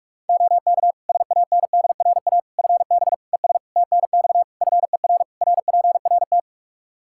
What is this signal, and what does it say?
Morse code, heard as a keyed tone, about 42 wpm, 700 Hz. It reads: OK SANDRA FB ES TNX FER RPRT
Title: Practice QSO - Exchange 2 - Info - 45wpm